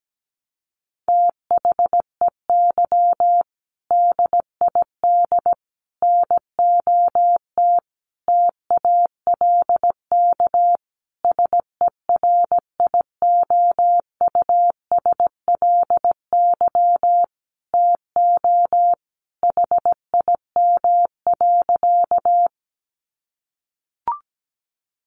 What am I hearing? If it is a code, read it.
Morse code, heard as a keyed tone, about 17 wpm, 700 Hz. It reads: THEY DID NOT TALK SERIOUSLY TO HIM.
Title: Sentences from Top 1200 Words 17wpm